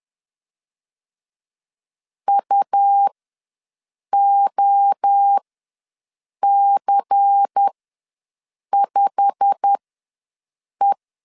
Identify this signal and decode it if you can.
Morse code, heard as a keyed tone, about 10 wpm, spaced as for 6 wpm, 800 Hz. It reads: UOC5E
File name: LEZIONI / 8 YJ45 / 8 YJ45 ws 10wpm 3843